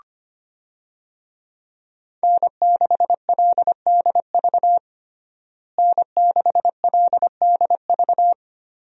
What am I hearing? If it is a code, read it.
Morse code, heard as a keyed tone, about 25 wpm, 700 Hz. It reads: N6LDV N6LDV
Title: USA Call Signs - 5 Characters - Repeated 2x - 25wpm